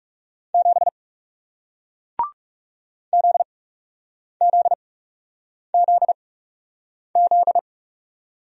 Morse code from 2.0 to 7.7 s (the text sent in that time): ZZZZ